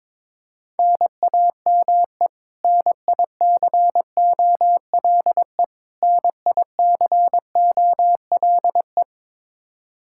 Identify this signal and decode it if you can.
Morse code, heard as a keyed tone, about 22 wpm, 700 Hz. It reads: NAME NICOLE NICOLE
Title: Practice QSO - Exchange 1 - Beginning - 22wpm